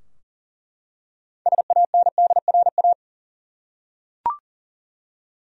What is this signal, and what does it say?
Morse code, heard as a keyed tone, about 40 wpm, 700 Hz. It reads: SANDRA
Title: USA Common Names - 40wpm